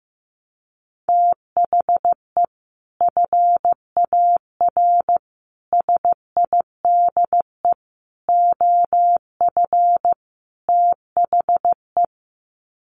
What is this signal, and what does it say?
Morse code, heard as a keyed tone, about 15 wpm, 700 Hz. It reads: THE FAR SIDE OF THE
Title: Sets of 5 Words 15wpm